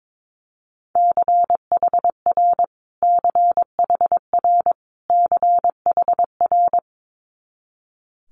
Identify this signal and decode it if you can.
Morse code, heard as a keyed tone, about 22 wpm, 700 Hz. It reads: CHR CHR CHR
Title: Multi-letter Phonemes Repeated 3x 22wpm